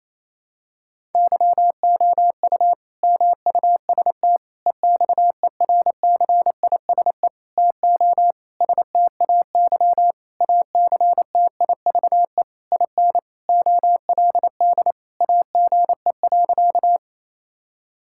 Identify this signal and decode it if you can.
Morse code, heard as a keyed tone, about 28 wpm, 700 Hz. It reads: YOU MUST EXERCISE TO STAY ACTIVE IN OLD AGE.
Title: Sentences from Top 1000 Words - Review Part 6 - 28wpm